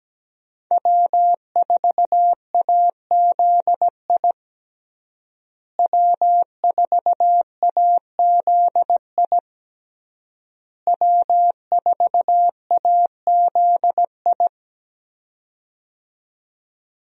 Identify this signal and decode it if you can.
Morse code, heard as a keyed tone, about 17 wpm, 700 Hz. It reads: W4AZI W4AZI W4AZI
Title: USA Call Signs - 5 Characters - Repeated 3x - 17wpm